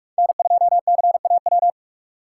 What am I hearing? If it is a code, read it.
Morse code, heard as a keyed tone, about 42 wpm, 700 Hz. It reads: N2CAW